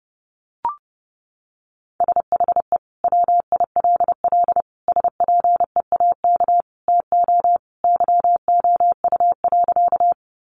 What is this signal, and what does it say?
Morse code, heard as a keyed tone, about 30 wpm, 700 Hz. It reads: SHE WILL SPEAK TO YOU.
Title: Sentences from Top 1100 Words 30wpm